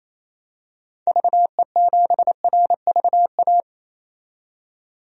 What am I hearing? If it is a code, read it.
Morse code, heard as a keyed tone, about 28 wpm, 700 Hz. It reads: VE7RVA
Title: DXCC-100 Call Signs - 6 Characters - 28wpm